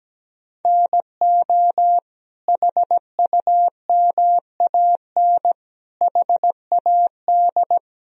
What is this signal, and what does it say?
Morse code, heard as a keyed tone, about 17 wpm, 700 Hz. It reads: NO HUMAN HAD
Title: Sentences from Top 1000 Words - Review Part 3 - 17wpm